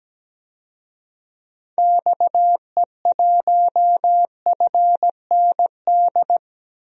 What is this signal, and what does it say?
Morse code, heard as a keyed tone, about 17 wpm, 700 Hz. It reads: XE1FND
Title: DXCC-100 Call Signs - 6 Characters - Repeated 3x - 17wpm